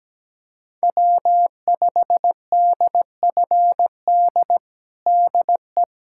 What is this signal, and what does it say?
Morse code, heard as a keyed tone, about 17 wpm, 700 Hz. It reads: W5DFD DE
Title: USA Callsign DE Callsign - 5 Characters - 17wpm